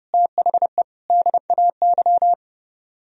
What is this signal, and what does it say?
Morse code, heard as a keyed tone, about 36 wpm, 700 Hz. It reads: THE DAY
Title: Speed-Racing - Sets of 3 Words 30wpm